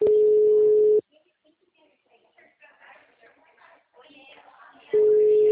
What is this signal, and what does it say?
Ringback tone (Europe, 425 Hz), with speech.